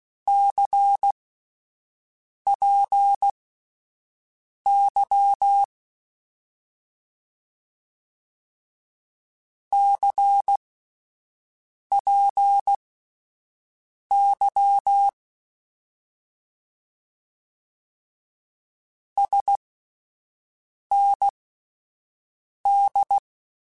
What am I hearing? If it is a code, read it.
Morse code, heard as a keyed tone, about 16 wpm, spaced as for 5 wpm, 800 Hz. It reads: CPY CPY SND
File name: Abreviations / Abreviations mp3s / Abreviations@5